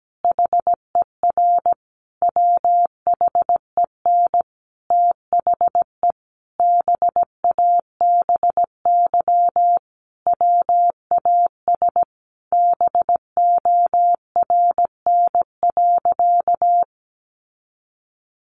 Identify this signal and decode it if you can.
Morse code, heard as a keyed tone, about 17 wpm, 700 Hz. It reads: HER WHEN THE BABY WAS BORN.
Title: Sentences from Top 1300 Words 17wpm